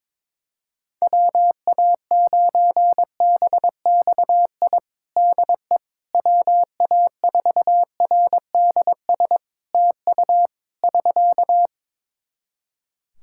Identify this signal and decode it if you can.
Morse code, heard as a keyed tone, about 22 wpm, 700 Hz. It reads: WA9BXI DE WA4RDS TU <SK>
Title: Practice QSO - Exchange 4 - Ending - 22wpm